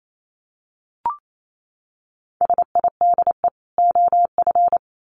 Morse code, heard as a keyed tone, about 28 wpm, 700 Hz. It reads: SIDE OF
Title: Sets of 2 Words 28wpm